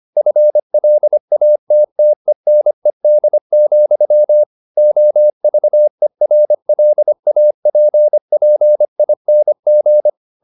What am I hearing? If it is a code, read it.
Morse code, heard as a keyed tone, about 25 wpm, 600 Hz. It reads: FLATTENED, OVERLAPPING
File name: the old man and the sea part 1 25wpm